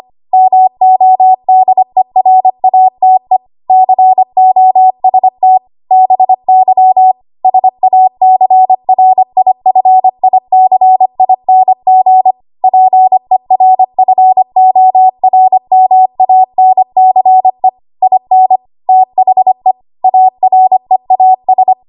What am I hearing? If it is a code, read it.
Morse code, heard as a keyed tone, about 25 wpm, 750 Hz. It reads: MODERATE COST BY SACRIFICING PERFORMANCE IN THE AREAS
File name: Archive / 25 WPM / 25WPM